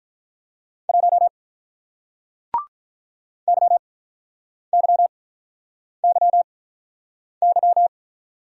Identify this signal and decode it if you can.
Morse code, heard as a keyed tone, about 50 wpm, spaced as for 9 wpm, 700 Hz. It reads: 1 YYYY